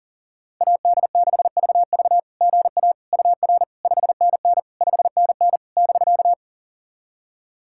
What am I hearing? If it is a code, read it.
Morse code, heard as a keyed tone, about 38 wpm, 700 Hz. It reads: AD6VV GA UR 5NN 5NN <BK>
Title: SOTA QSO Practice 40wpm